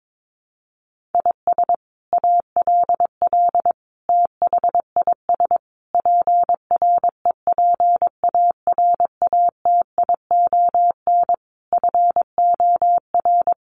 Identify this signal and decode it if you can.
Morse code, heard as a keyed tone, about 22 wpm, 700 Hz. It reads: IS ALL THIS PREPARATION FOR
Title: Sentences from Top 1400 Words 22wpm